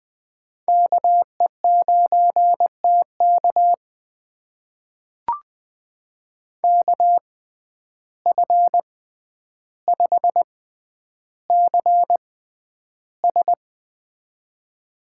Morse code, heard as a keyed tone, about 20 wpm, spaced as for 10 wpm, 700 Hz. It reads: KE9TK KF5CS
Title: USA Call Signs - 5 Characters - 6x Character Spacing - 20wpm